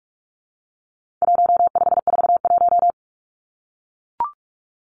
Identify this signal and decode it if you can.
Morse code, heard as a keyed tone, about 44 wpm, 700 Hz. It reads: 1541
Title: Numbers - 4-Digit 45wpm